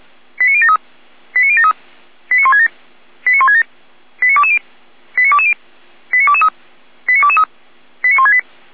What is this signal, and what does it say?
Five-tone selective call (ZVEI), heard as quick runs of five tones, tones of about 70 ms. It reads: 89073 89073 89167 89167 89220 89220 89223 89223 89168
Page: SDR/ZVEI